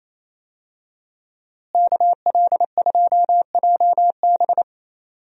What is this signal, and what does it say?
Morse code, heard as a keyed tone, about 27 wpm, 700 Hz. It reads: KL2JB
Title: USA Call Signs - 5 Characters - Repeated 3x - 28wpm